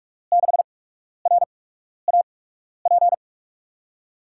Morse code, heard as a keyed tone, about 44 wpm, spaced as for 12 wpm, 700 Hz. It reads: 6RAP
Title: USA Call Signs - 5 Characters - 8x Character Spacing - 45wpm